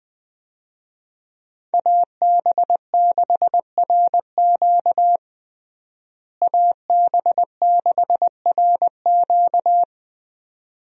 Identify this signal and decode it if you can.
Morse code, heard as a keyed tone, about 20 wpm, 700 Hz. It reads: AB6RQ AB6RQ
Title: USA Call Signs - 5 Characters - Repeated 3x - 20wpm